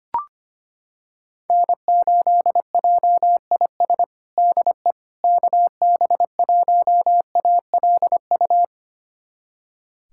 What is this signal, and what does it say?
Morse code, heard as a keyed tone, about 25 wpm, 700 Hz. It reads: N8JIS DE KB1ALU
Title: USA Callsign DE Callsign 25wpm